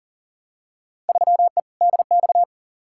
Morse code, heard as a keyed tone, about 37 wpm, 700 Hz. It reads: 3E DX
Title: Field Day Exchanges Encore - 40wpm